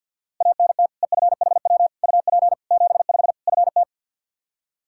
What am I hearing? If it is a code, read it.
Morse code, heard as a keyed tone, about 49 wpm, 700 Hz. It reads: ANT EFHW UP 75 FT
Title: Practice QSO - Exchange 2 - Info - 50wpm